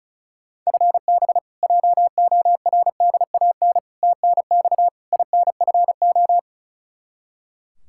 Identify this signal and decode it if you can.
Morse code, heard as a keyed tone, about 36 wpm, 700 Hz. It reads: FB JORDAN TNX INFO
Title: Practice QSO - Exchange 3 - More Info - 35wpm